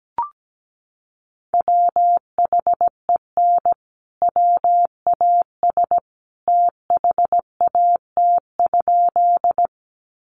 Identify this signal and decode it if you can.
Morse code, heard as a keyed tone, about 17 wpm, 700 Hz. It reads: WHEN WAS THAT?
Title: Sentences from Top 100 Words 17wpm